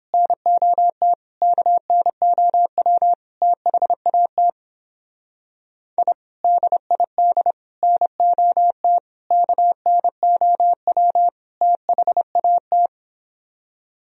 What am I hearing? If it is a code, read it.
Morse code, heard as a keyed tone, about 31 wpm, 700 Hz. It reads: NOT KNOW THAT I DID NOT KNOW THAT